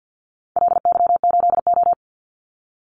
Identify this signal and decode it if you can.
Morse code, heard as a keyed tone, about 50 wpm, 700 Hz. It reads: LY8O
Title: Call Signs - 50wpm